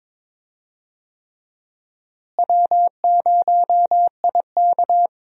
Morse code, heard as a keyed tone, about 22 wpm, 700 Hz. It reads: W0IK
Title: USA Call Signs - 4 Characters - Repeated 3x - 22wpm